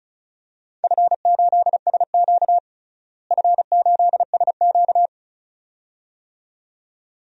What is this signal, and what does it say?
Morse code, heard as a keyed tone, about 36 wpm, 700 Hz. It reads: F8SQ F8SQ
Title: DXCC-100 Call Signs - 4 Characters - Repeated 2x - 35wpm